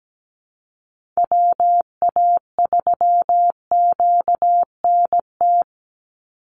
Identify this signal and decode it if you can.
Morse code, heard as a keyed tone, about 17 wpm, 700 Hz. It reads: WA3QNT
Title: Call Signs Repeated 3x - 17wpm